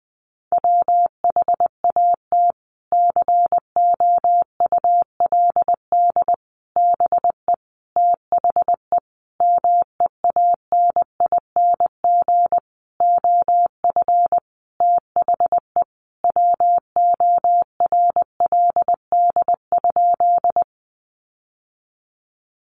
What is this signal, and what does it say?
Morse code, heard as a keyed tone, about 19 wpm, 700 Hz. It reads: WHAT COULD BE THE MEANING OF THE WORLD?
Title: Sentences from Top 900 Words 20wpm